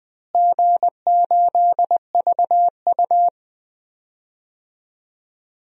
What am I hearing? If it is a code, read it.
Morse code, heard as a keyed tone, about 20 wpm, 700 Hz. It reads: G8VU
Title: DXCC-100 Call Signs - 4 Characters - Repeated 2x - 20wpm